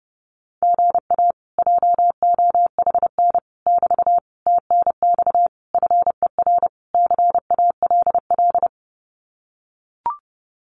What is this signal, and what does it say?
Morse code, heard as a keyed tone, about 30 wpm, 700 Hz. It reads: GA JOHN <BT> TNX FER CALL
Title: Common QSO Phrases Encore 30wpm